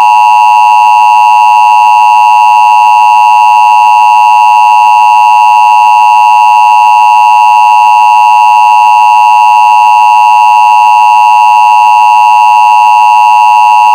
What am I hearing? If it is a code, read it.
Emergency Alert System (SAME) tones (the 853 and 960 Hz attention tone), with noise. It is heard from 1.0 to 13.2 s.
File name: Emergency Alert System Attention Signal 30s